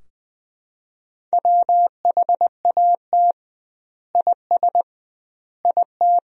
Morse code, heard as a keyed tone, about 20 wpm, 700 Hz. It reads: WHAT IS IT
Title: 2x Word-Spacing - Sentences from Top 100 Words - 20wpm